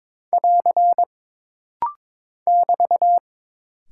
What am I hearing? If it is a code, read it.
Morse code, heard as a keyed tone, about 22 wpm, 700 Hz. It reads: <AR> <BT>
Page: This rapid-fire timing is geared to push your ICR (Instant Character Recognition) skills.